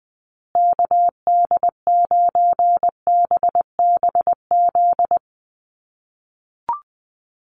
Morse code, heard as a keyed tone, about 20 wpm, 700 Hz. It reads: KD9BBZ
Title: USA Call Signs Repeated 3x 20wpm